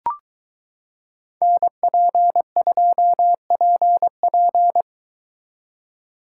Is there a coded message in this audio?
Yes: Morse code from 1.4 to 4.8 s, heard as a keyed tone, about 23 wpm, 700 Hz. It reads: NP2PP